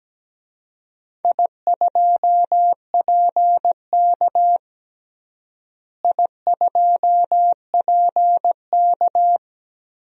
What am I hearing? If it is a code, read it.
Morse code, heard as a keyed tone, about 17 wpm, 700 Hz. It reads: I2PK I2PK
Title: DXCC-100 Call Signs - 4 Characters - Repeated 3x - 17wpm